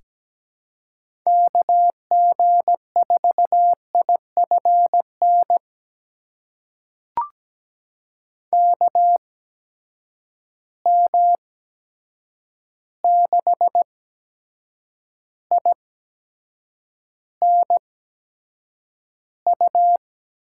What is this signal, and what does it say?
Morse code, heard as a keyed tone, about 17 wpm, spaced as for 7 wpm, 700 Hz. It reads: KG4IFN KM6INU
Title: USA Call Signs - 6 Characters - 8x Character Spacing - 17wpm